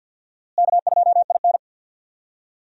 Morse code, heard as a keyed tone, about 50 wpm, 700 Hz. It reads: K2IN
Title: USA Call Signs - 4 Characters - Repeated 2x - 50wpm